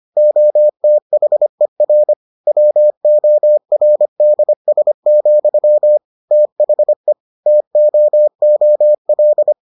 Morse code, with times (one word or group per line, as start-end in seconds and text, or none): 0.2-2.1 OTHER
2.5-6.0 WORDS,
6.3-7.1 THE
7.5-9.5 TOOL